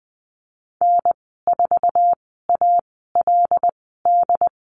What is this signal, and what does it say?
Morse code, heard as a keyed tone, about 20 wpm, spaced as for 14 wpm, 700 Hz. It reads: N4ALD
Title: 082 Lesson 14g - Number 2 - Call Signs 20wpm